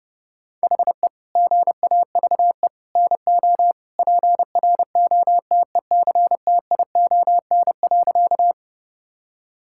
Morse code, heard as a keyed tone, about 30 wpm, 700 Hz. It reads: HE GAVE NO PROTECTION.
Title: Sentences from Top 1600 Words 30wpm